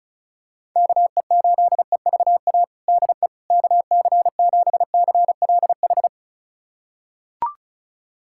Morse code, heard as a keyed tone, about 36 wpm, 700 Hz. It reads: KE8EVA DE KC7CLH
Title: USA Callsign DE Callsign 35wpm